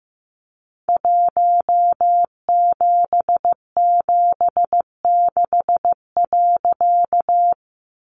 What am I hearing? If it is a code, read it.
Morse code, heard as a keyed tone, about 15 wpm, 700 Hz. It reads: 1776.
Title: A Sound Of Thunder 15wpm